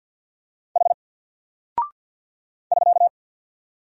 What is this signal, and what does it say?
Morse code, heard as a keyed tone, about 48 wpm, 700 Hz. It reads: H <SK>
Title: Single Character (Letter:Number:Punctuation:Prosign) Alt 50wpm